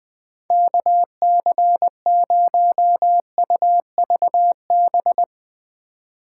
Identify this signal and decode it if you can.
Morse code, heard as a keyed tone, about 20 wpm, 700 Hz. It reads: KC0UVB
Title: Speed Racing - USA 2x3 Call Signs 17wpm